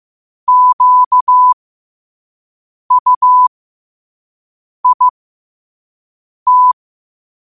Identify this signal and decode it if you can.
Morse code, heard as a keyed tone, about 15 wpm, spaced as for 5 wpm, 1000 Hz. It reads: QUIT
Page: New quotes every day in morse code at 5 Words per minute.